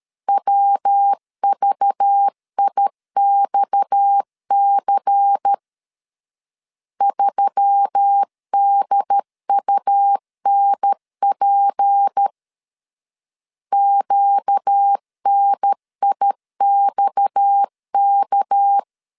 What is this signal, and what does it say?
Morse code, heard as a keyed tone, about 13 wpm, 800 Hz. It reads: WVIXC 3DUNP QNIXK
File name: CW / DET / 1 6 1500C S A 12wpm 3440